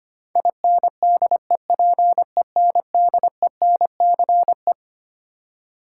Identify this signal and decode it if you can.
Morse code, heard as a keyed tone, about 25 wpm, 700 Hz. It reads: INDEPENDENCE
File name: Twelve Letter Words - 25wpm